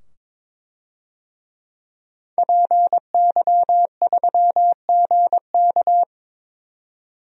Morse code, heard as a keyed tone, about 22 wpm, 700 Hz. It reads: PY3GK
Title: DXCC-100 Call Signs - 5 Characters - 22wpm